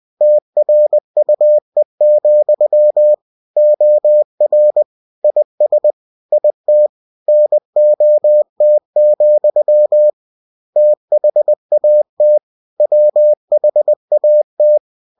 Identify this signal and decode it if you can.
Morse code, heard as a keyed tone, about 20 wpm, 600 Hz. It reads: TRUE, OR IS IT NOT, THAT WHAT
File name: cw-books / winnie-the-pooh / winnie-20wpm-page-0068